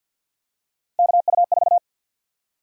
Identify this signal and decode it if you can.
Morse code, heard as a keyed tone, about 52 wpm, 700 Hz. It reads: KU4
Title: USA Call Sign Prefixes - Repeated 2x - 50wpm